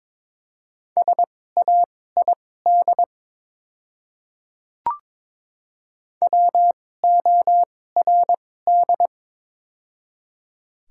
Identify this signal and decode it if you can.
Morse code, heard as a keyed tone, about 22 wpm, spaced as for 16 wpm, 700 Hz. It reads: SAID WORD
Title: 2x Character Spacing - Top 100 Words - 22wpm